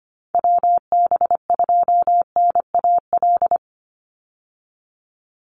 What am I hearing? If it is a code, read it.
Morse code, heard as a keyed tone, about 25 wpm, 700 Hz. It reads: WB2NAL
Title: USA Call Signs 25wpm